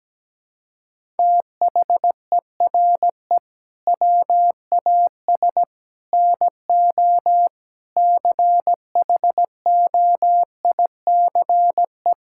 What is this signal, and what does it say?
Morse code, heard as a keyed tone, about 17 wpm, 700 Hz. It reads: THERE WAS NO CHOICE
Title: Sentences from Top 1000 Words - Review Part 6 - 17wpm